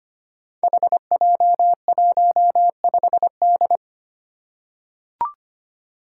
Morse code, heard as a keyed tone, about 25 wpm, 700 Hz. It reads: HJ15D
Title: Five Challenging Letter-Number Combinations 25wpm